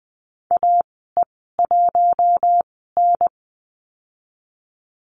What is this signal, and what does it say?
Morse code, heard as a keyed tone, about 20 wpm, spaced as for 14 wpm, 700 Hz. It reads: AE1N
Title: 042 Lesson 8g - Number 1 - Call Signs 20wpm